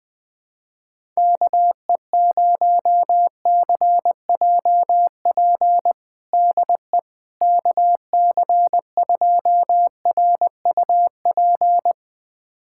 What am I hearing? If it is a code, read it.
Morse code, heard as a keyed tone, about 20 wpm, 700 Hz. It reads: KE0CJP DE KC2RUP
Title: USA Callsign DE Callsign - 6 Characters - 20wpm